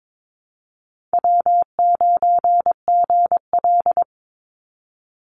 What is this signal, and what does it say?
Morse code, heard as a keyed tone, about 22 wpm, 700 Hz. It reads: W9GL
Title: Call Signs Repeated 3x - 22wpm